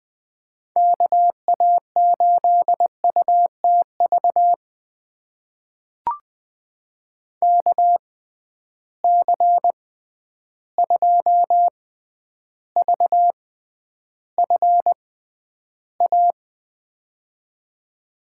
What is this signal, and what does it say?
Morse code, heard as a keyed tone, about 20 wpm, spaced as for 10 wpm, 700 Hz. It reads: KA8UTV KC2VFA
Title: USA Call Signs - 6 Characters - 6x Character Spacing - 20wpm